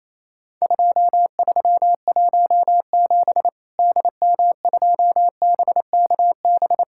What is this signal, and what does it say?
Morse code, heard as a keyed tone, about 28 wpm, 700 Hz. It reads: 2317 DM2BKB
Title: Worked All Europe DX Contest - QTC Exchange - 28wpm